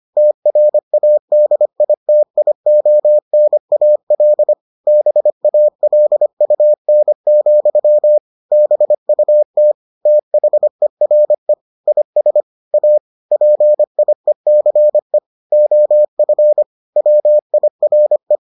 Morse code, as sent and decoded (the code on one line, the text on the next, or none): - .-. .- -.. .. - .. --- -. .- .-.. / -... .- .-.. ..- -. --..-- / -... ..- - / - .... .-. / .. ... / .- / .--. .. -.-. / --- ..-. / .-- .. .-. .
TRADITIONAL BALUN, BUT THERE IS A PIECE OF WIRE